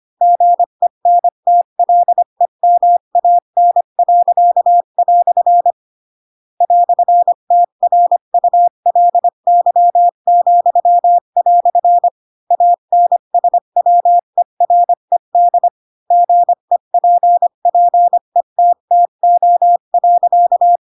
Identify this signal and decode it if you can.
Morse code, heard as a keyed tone, about 25 wpm, 700 Hz. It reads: GENTLEMAN." "TRULY," ANSWERED GEPPETTO.
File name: Pinocchio / Pinocchio ch8 20 25 ik1hge